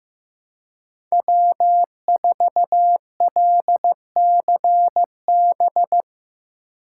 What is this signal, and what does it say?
Morse code, heard as a keyed tone, about 17 wpm, 700 Hz. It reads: W4LCB
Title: Speed Racing - USA 1x3 Call Signs 15wpm